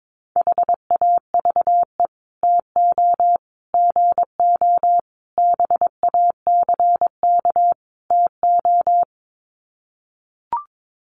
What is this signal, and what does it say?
Morse code, heard as a keyed tone, about 22 wpm, 700 Hz. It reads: HAVE TO GO BACK TO
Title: Sets of 5 Words Encore2 22wpm